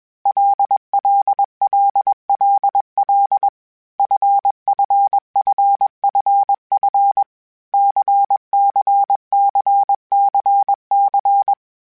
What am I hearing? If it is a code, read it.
Morse code, heard as a keyed tone, about 21 wpm, 800 Hz. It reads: LLLLL FFFFF CCCCC
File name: Basic Sets / Basic Letter Groups Collective7 / LFCYQ@20